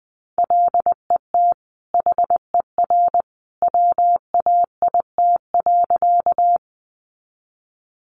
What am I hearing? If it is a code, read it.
Morse code, heard as a keyed tone, about 20 wpm, 700 Hz. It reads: LET HER WAIT.
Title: Sentences from Top 500 Words 20wpm